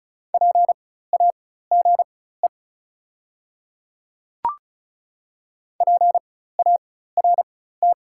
Morse code, heard as a keyed tone, about 36 wpm, spaced as for 16 wpm, 690 Hz. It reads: PAGE PART
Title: 4x Character Spacing - Top 300 Words - 35wpm